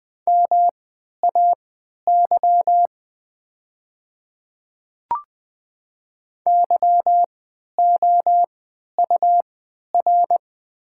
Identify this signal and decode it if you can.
Morse code, heard as a keyed tone, about 20 wpm, spaced as for 11 wpm, 700 Hz. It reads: MAY YOUR
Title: 3x Character Spacing - Top 500 Words - 20wpm